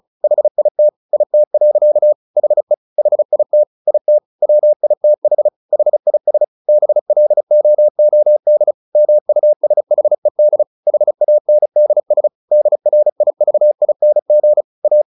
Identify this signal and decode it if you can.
Morse code, heard as a keyed tone, about 34 wpm, 600 Hz. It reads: HIT IT. HE HIT IT WITH HIS BLOOD MUSHED HANDS DRIVING A
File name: the old man and the sea 2 35wpm